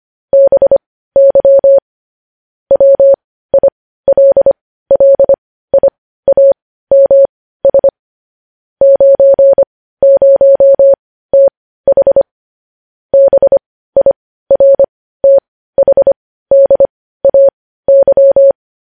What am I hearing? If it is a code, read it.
Morse code, heard as a keyed tone, about 25 wpm, spaced as for 15 wpm, 550 Hz. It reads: BY WILLIAMS 90TH BIRTHDAY
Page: news-headlines-in-morse-code-at-15-wpm